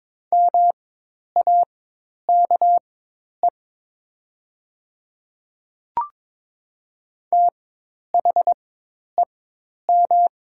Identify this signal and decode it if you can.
Morse code, heard as a keyed tone, about 22 wpm, spaced as for 10 wpm, 700 Hz. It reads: MAKE THEM
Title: 4x Character Spacing - Top 100 Words - 22wpm